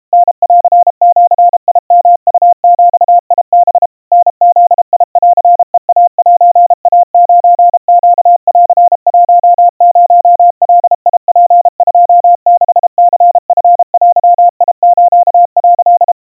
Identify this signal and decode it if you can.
Morse code (Japanese Wabun code), heard as a keyed tone, about 32 wpm, 700 Hz. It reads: タンジヨウビハ タブンヘイセイ9ネン10ガツ26ニチデス」
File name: kawa / Morse sample / kamyu 120 cpm 700 Hz Crispy